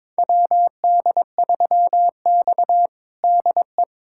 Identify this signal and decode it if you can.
Morse code, heard as a keyed tone, about 23 wpm, 700 Hz. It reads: WD3X DE
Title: USA Callsign DE Callsign - 4 Characters - 22wpm